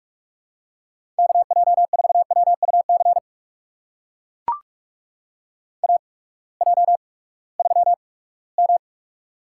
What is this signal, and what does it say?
Morse code, heard as a keyed tone, about 44 wpm, spaced as for 22 wpm, 690 Hz. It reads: KJ4WUC AJ3M
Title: USA Call Signs - 8x Character Spacing - 45wpm